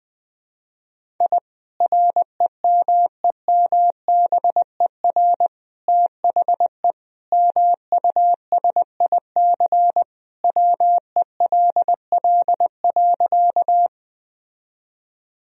Morse code, heard as a keyed tone, about 20 wpm, 700 Hz. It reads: I REMEMBER THE MUSIC WELL.
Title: Sentences from Top 1000 Words - Review Part 1 - 20wpm